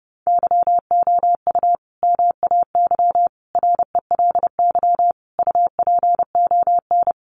Morse code, heard as a keyed tone, about 30 wpm, 700 Hz. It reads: YOU MAY RELY UPON
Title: Sentences from Top 1200 Words 30wpm